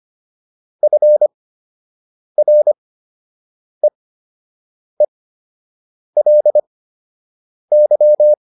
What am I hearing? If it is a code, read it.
Morse code, heard as a keyed tone, about 25 wpm, spaced as for 7 wpm, 600 Hz. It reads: FREELY
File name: LY6 07